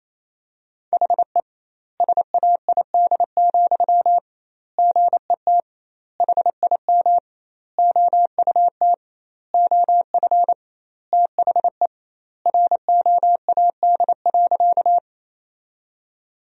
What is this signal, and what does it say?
Morse code, heard as a keyed tone, about 28 wpm, 700 Hz. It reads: HE SAID, GET HIM OUT OF THE ROAD.
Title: 2x Word-Spacing - Sentences from Top 400 Words - 28wpm